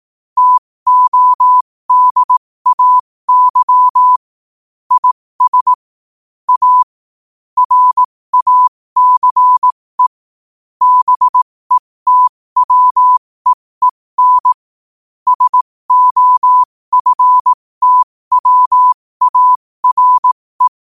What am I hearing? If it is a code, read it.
Morse code, heard as a keyed tone, about 18 wpm, spaced as for 15 wpm, 1000 Hz. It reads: TODAY IS A RACE BETWEEN SOFTWARE
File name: qotdfiles / 15wpm